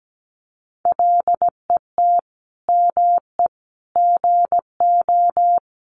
Morse code, heard as a keyed tone, about 17 wpm, 700 Hz. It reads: LET ME GO
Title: Sentences from Top 300 Words 17wpm